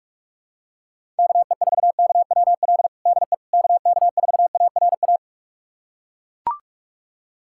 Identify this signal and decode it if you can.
Morse code, heard as a keyed tone, about 41 wpm, 700 Hz. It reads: KE4KWL DE KK4ARA
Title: USA Callsign DE Callsign - 6 Characters - 45wpm